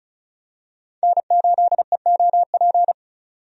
Morse code, heard as a keyed tone, about 36 wpm, 700 Hz. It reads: N8EOP
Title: USA Call Signs - 5 Characters - 35wpm